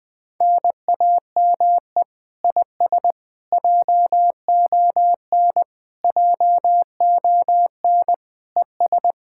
Morse code, heard as a keyed tone, about 20 wpm, 700 Hz. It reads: NAME IS JON JON ES
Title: Speed-Racing Common QSO Phrases - 17wpm